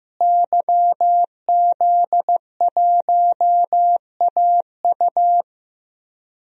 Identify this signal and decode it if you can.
Morse code, heard as a keyed tone, about 15 wpm, 700 Hz. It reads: YZ1AU
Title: DXCC-100 Call Signs - 5 Characters - Repeated 2x - 15wpm